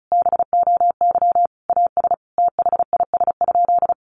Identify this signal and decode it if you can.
Morse code, heard as a keyed tone, about 34 wpm, 700 Hz. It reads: BOY AS THIS?
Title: Sentences from Top 400 Words 35wpm